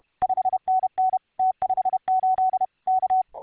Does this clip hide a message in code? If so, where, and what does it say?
Yes: Morse code from 0.2 to 3.2 s, heard as a keyed tone, about 32 wpm, 750 Hz. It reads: 5NN T58 K